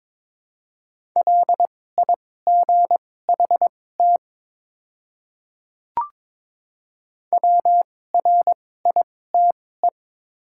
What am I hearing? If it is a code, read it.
Morse code, heard as a keyed tone, about 22 wpm, spaced as for 16 wpm, 700 Hz. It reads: LIGHT WRITE
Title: 2x Character Spacing - Top 200 Words - 22wpm